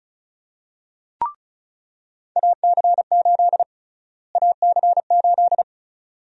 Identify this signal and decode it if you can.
Morse code, heard as a keyed tone, about 36 wpm, 700 Hz. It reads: AC8 AC8
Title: USA Call Sign Prefixes - Repeated 2x - 35wpm